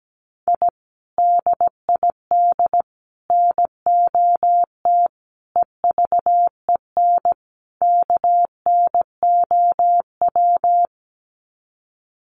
Morse code, heard as a keyed tone, about 17 wpm, 700 Hz. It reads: I DID NOT EVEN KNOW
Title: Sets of 5 Words Encore 17wpm